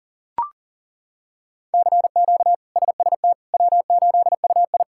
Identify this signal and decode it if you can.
Morse code, heard as a keyed tone, about 40 wpm, 700 Hz. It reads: CQ SST W8UI
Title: K1USN SST QSO Practice - 40wpm